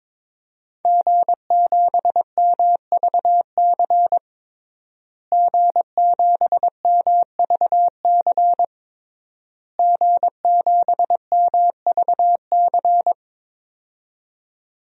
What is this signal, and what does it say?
Morse code, heard as a keyed tone, about 22 wpm, 700 Hz. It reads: G7MVC G7MVC G7MVC
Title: DXCC-100 Call Signs - 5 Characters - Repeated 3x - 22wpm